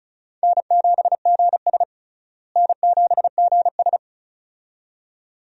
Morse code, heard as a keyed tone, about 36 wpm, 700 Hz. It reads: N7GS N7GS
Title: USA Call Signs - 4 Characters - Repeated 2x - 35wpm